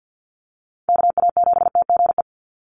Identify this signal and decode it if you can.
Morse code, heard as a keyed tone, about 50 wpm, 700 Hz. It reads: KA7TGE
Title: USA Call Signs 50wpm